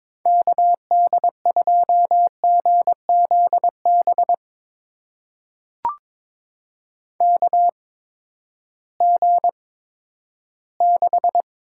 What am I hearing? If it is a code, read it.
Morse code, heard as a keyed tone, about 22 wpm, spaced as for 13 wpm, 700 Hz. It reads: KD2GZB KG6
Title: USA Call Signs - 6 Characters - 8x Character Spacing - 22wpm